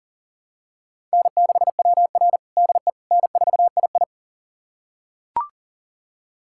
Morse code, heard as a keyed tone, about 37 wpm, 700 Hz. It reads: N6WR DE N4II
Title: USA Callsign DE Callsign - 4 Characters - 40wpm